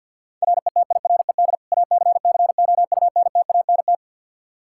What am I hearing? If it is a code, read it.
Morse code, heard as a keyed tone, about 50 wpm, 700 Hz. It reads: RETIRED ACCOUNTANT
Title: Practice QSO - Exchange 3 - More Info - 50wpm